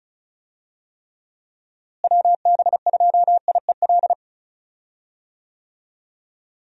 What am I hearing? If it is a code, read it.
Morse code, heard as a keyed tone, about 34 wpm, 700 Hz. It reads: WB2IEL
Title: USA Call Signs - 6 Characters - Repeated 2x - 35wpm